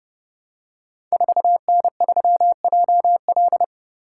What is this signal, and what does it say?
Morse code, heard as a keyed tone, about 28 wpm, 700 Hz. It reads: 4N3JL
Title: Code Groups - 5 Letter:Numbers - 30wpm